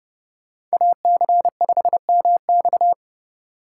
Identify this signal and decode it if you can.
Morse code, heard as a keyed tone, about 38 wpm, 700 Hz. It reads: AC5MX